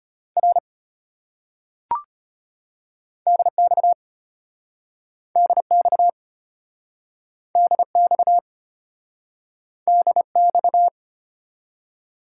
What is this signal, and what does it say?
Morse code, heard as a keyed tone, about 33 wpm, 700 Hz. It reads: R DX DX DX DX